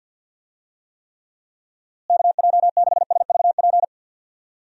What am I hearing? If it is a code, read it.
Morse code, heard as a keyed tone, about 50 wpm, 700 Hz. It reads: KJ6SVP
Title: USA Call Signs - 6 Characters - Repeated 2x - 50wpm